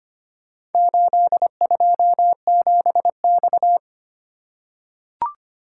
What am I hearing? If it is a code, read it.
Morse code, heard as a keyed tone, about 25 wpm, 700 Hz. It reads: 827X
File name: practice / 25wpm / Four Letter-Number Combinations - 25wpm